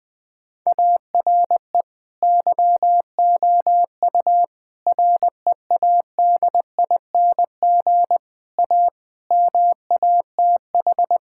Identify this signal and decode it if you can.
Morse code, heard as a keyed tone, about 20 wpm, 700 Hz. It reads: ARE YOU READING A MATH
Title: Sentences from Top 1000 Words - Review Part 6 - 20wpm